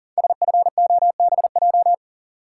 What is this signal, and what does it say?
Morse code, heard as a keyed tone, about 42 wpm, 700 Hz. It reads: SFOBJ
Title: Code Groups - 5 Letters - 40wpm